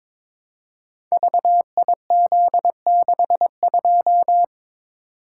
Morse code, heard as a keyed tone, about 23 wpm, 700 Hz. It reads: VIZ62
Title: Code Groups - 5 Letter:Numbers - 22wpm